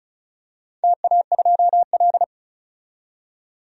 Morse code, heard as a keyed tone, about 36 wpm, 700 Hz. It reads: TA2L
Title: DXCC-100 Call Signs - 4 Characters - 35wpm